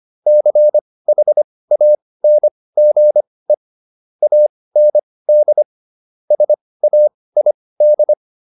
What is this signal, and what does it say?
Morse code, heard as a keyed tone, about 25 wpm, spaced as for 18 wpm, 600 Hz. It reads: CHANGE AND SAID